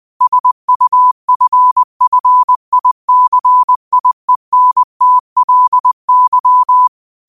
Morse code, heard as a keyed tone, about 20 wpm, 1000 Hz. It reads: SUFFICIENTLY